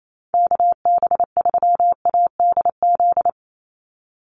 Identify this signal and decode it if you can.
Morse code, heard as a keyed tone, about 28 wpm, 700 Hz. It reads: KB3ADZ
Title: USA Call Signs 28wpm